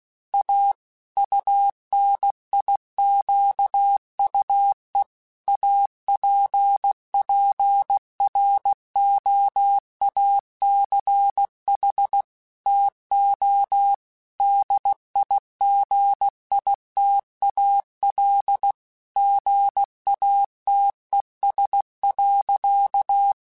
Morse code, heard as a keyed tone, about 16 wpm, 800 Hz. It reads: A UNIQUE APPROACH TO DIGITAL GATES.
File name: Technical Subjects / Technical Subjects-mp3s / Gates Intro @15